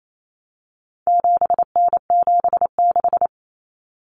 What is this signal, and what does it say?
Morse code, heard as a keyed tone, about 28 wpm, 700 Hz. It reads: 7N76
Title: Common Cut Numbers - 4-Digit 28wpm